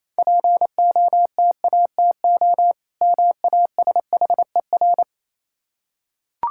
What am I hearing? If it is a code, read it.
Morse code, heard as a keyed tone, about 26 wpm, 690 Hz. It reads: POTATO MASHER
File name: Easy Words - Speed-Racing with Hints - 22wpm